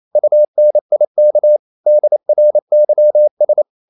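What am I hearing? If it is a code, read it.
Morse code, heard as a keyed tone, about 28 wpm, 600 Hz. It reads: UNIK DRYS
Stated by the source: Danish short words sent in Morse at 28 wpm, from CW med Gnister.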